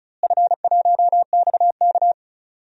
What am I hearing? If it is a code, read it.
Morse code, heard as a keyed tone, about 36 wpm, 700 Hz. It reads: F1XK
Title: DXCC-100 Call Signs - 4 Characters - Repeated 2x - 35wpm